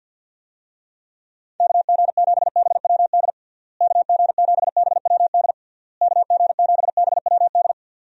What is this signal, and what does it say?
Morse code, heard as a keyed tone, about 52 wpm, 700 Hz. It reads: KG7BWD KG7BWD KG7BWD
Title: USA Call Signs - 6 Characters - Repeated 3x - 50wpm